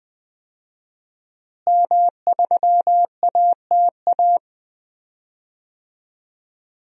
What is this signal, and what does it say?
Morse code, heard as a keyed tone, about 20 wpm, 700 Hz. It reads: M3ATA
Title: DXCC-100 Call Signs - 5 Characters - Repeated 3x - 20wpm